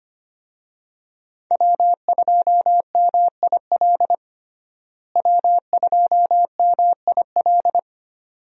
Morse code, heard as a keyed tone, about 25 wpm, 700 Hz. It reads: W2MIL W2MIL
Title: USA Call Signs - 5 Characters - Repeated 2x - 25wpm